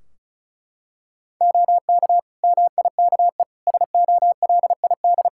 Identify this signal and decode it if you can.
Morse code, heard as a keyed tone, about 36 wpm, 700 Hz. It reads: OK MIKE SOLID
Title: Practice QSO - Exchange 3 - More Info - 35wpm